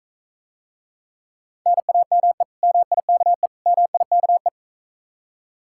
Morse code, heard as a keyed tone, about 45 wpm, 700 Hz. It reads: NAME MIKE MIKE